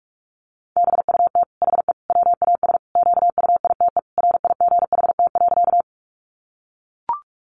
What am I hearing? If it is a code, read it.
Morse code, heard as a keyed tone, about 44 wpm, 700 Hz. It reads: BUT HE WAS QUITE RIGHT.
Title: Sentences from Top 700 Words 45wpm